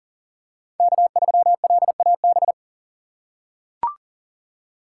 Morse code, heard as a keyed tone, about 40 wpm, 700 Hz. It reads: K3LAB
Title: USA Call Signs - 5 Characters - 3x Character Spacing - 40wpm